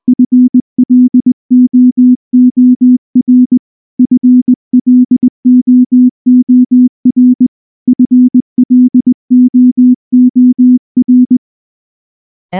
Morse code, heard as a keyed tone, about 19 wpm, 250 Hz. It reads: FLOOR FLOOR FLOOR
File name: morse / morse-training-20WPM-10